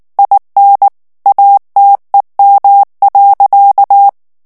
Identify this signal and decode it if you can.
Morse code, heard as a keyed tone, about 19 wpm, 800 Hz. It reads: IN ATEM.
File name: morse mp3 / ALLEMAND texte clair 2 vitesse 18 mots minute